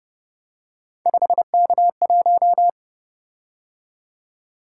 Morse code, heard as a keyed tone, about 29 wpm, 700 Hz. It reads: 5K1